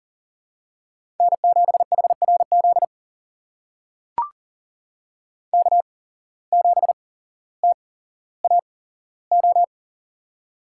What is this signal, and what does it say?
Morse code, heard as a keyed tone, about 40 wpm, spaced as for 17 wpm, 700 Hz. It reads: N7HRZ K7TAO
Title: USA Call Signs - 5 Characters - 8x Character Spacing - 40wpm